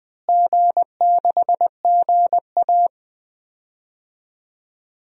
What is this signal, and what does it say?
Morse code, heard as a keyed tone, about 20 wpm, 700 Hz. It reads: G6GA
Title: DXCC-100 Call Signs - 4 Characters - Repeated 3x - 20wpm